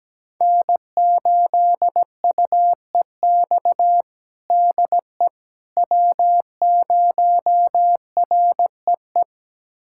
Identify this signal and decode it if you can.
Morse code, heard as a keyed tone, about 17 wpm, 700 Hz. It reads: N8UEX DE W0REE
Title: USA Callsign DE Callsign - 5 Characters - 17wpm